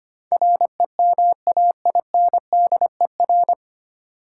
Morse code, heard as a keyed tone, about 25 wpm, 700 Hz. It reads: REMAINDER
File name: Nine Letter Words - 25wpm